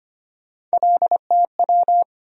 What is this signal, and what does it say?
Morse code, heard as a keyed tone, about 25 wpm, 700 Hz. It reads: LTW
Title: USA Call Sign Postfixes - 3 Letters 25wpm